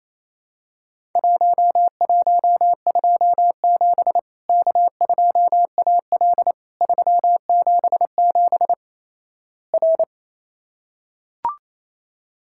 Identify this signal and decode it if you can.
Morse code, heard as a keyed tone, about 28 wpm, 700 Hz. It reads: 1127 K2AL 377 R
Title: Worked All Europe DX Contest - QTC Exchange - 28wpm